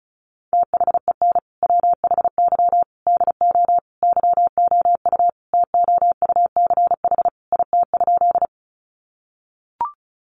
Morse code, heard as a keyed tone, about 34 wpm, 700 Hz. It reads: THEN WHY DO YOU TOUCH IT?
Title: Sentences from Top 1200 Words 35wpm